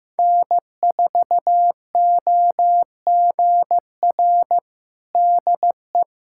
Morse code, heard as a keyed tone, about 15 wpm, 700 Hz. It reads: N4OGR DE
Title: USA Callsign DE Callsign - 5 Characters - 15wpm